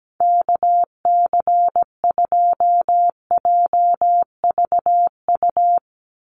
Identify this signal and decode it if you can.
Morse code, heard as a keyed tone, about 17 wpm, 700 Hz. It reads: KC2JVU
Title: USA Call Signs 17wpm